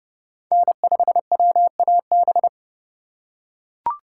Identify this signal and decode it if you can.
Morse code, heard as a keyed tone, about 31 wpm, 700 Hz. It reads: N5WAB
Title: USA Call Signs - 5 Characters - 2x Character Spacing - 30wpm